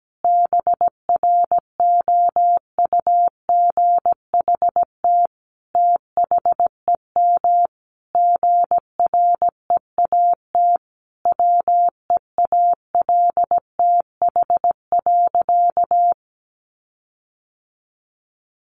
Morse code, heard as a keyed tone, about 17 wpm, 700 Hz. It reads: BROUGHT THEM GREAT WEALTH.
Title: Sentences from Top 1300 Words 17wpm